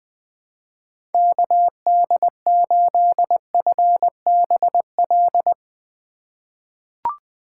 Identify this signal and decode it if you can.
Morse code, heard as a keyed tone, about 21 wpm, 700 Hz. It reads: KD8FBL
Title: USA Call Signs - 6 Characters - 2x Character Spacing - 20wpm